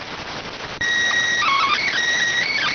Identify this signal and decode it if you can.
SSTV (slow-scan television), PD 290.